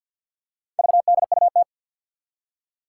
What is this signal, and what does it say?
Morse code, heard as a keyed tone, about 47 wpm, 700 Hz. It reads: VDUT
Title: Four Challenging Letter-Number Combinations 50wpm